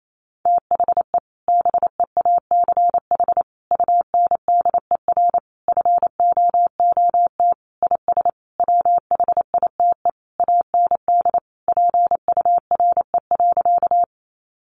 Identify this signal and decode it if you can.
Morse code, heard as a keyed tone, about 28 wpm, 700 Hz. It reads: THE BEACH UNDER FOOT IS WHITE AND PURE.
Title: Sentences from Top 1500 Words 28wpm